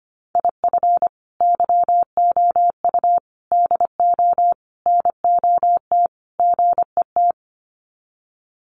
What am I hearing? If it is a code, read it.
Morse code, heard as a keyed tone, about 25 wpm, 700 Hz. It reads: IF YOU DO NOT GET
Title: Sets of 5 Words Encore2 25wpm